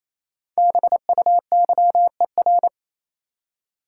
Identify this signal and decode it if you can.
Morse code, heard as a keyed tone, about 28 wpm, 700 Hz. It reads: BUYER
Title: Five Letter Words - 28wpm